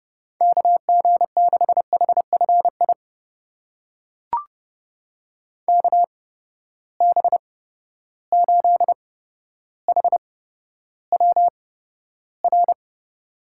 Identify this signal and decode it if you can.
Morse code, heard as a keyed tone, about 30 wpm, spaced as for 13 wpm, 700 Hz. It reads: KG6HFI KB8HWR
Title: USA Call Signs - 6 Characters - 8x Character Spacing - 30wpm